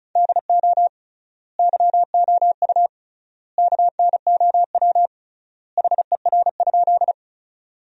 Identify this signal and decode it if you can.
Morse code, heard as a keyed tone, about 36 wpm, 700 Hz. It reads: DO YOU KNOW HER?
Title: 3x Word-Spacing - Sentences from Top 100 Words - 35wpm